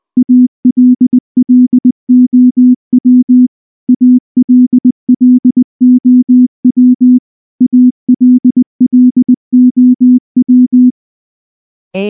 Morse code, heard as a keyed tone, about 19 wpm, 250 Hz. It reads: ALLOW ALLOW ALLOW
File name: morse / morse-training-20WPM-1